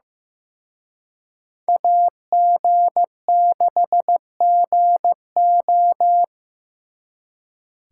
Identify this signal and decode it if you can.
Morse code, heard as a keyed tone, about 18 wpm, 700 Hz. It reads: AG6GO
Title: Speed Racing - USA Call Signs 15wpm